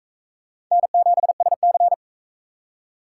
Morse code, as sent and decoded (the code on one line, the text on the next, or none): -. --... ... -.-.
N7SC